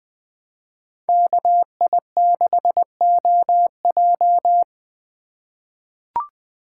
Morse code, heard as a keyed tone, about 20 wpm, 700 Hz. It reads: KI6OJ
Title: USA Call Signs - 5 Characters - 2x Character Spacing - 20wpm